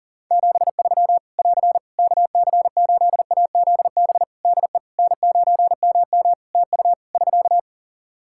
Morse code, heard as a keyed tone, about 40 wpm, 700 Hz. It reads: 73 <AR> KC8AZB DE N9MM TU <SK>
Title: K1USN SST QSO Practice 40wpm